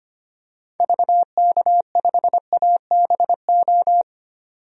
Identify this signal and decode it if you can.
Morse code, heard as a keyed tone, about 25 wpm, 700 Hz. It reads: VK5ABO
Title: DXCC-100 Call Signs - 6 Characters - 25wpm